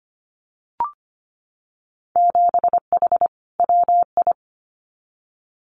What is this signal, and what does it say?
Morse code, heard as a keyed tone, about 25 wpm, 700 Hz. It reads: 7H WI
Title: Winter Field Day Exchanges Encore 25wpm